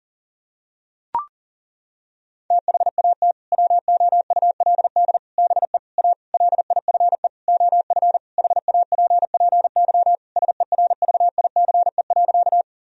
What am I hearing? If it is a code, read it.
Morse code, heard as a keyed tone, about 40 wpm, 700 Hz. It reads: THAT WOULD BE A LIFE OF HAPPY SERVICE.
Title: Sentences from Top 1000 Words - Review Part 4 - 40wpm